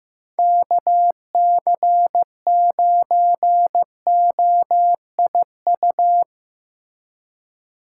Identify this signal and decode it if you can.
Morse code, heard as a keyed tone, about 15 wpm, 700 Hz. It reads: KC9OIU
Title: USA Call Signs - 6 Characters - 2x Character Spacing - 15wpm